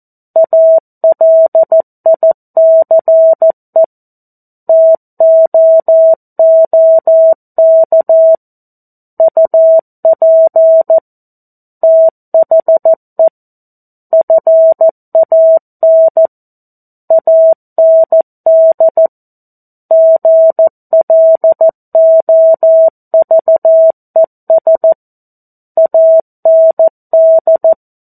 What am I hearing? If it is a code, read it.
Morse code, heard as a keyed tone, about 14 wpm, 650 Hz. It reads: ALICE TOOK UP THE FAN AND GLOVES AND
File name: 02-Chapter-02 14 WPM